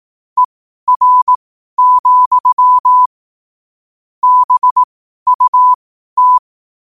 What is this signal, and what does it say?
Morse code, heard as a keyed tone, about 18 wpm, spaced as for 12 wpm, 1000 Hz. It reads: ER, BUT
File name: qotdfiles / 12wpm